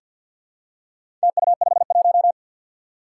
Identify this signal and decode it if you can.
Morse code, heard as a keyed tone, about 48 wpm, 700 Hz. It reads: TU51
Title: Four Challenging Letter-Number Combinations 50wpm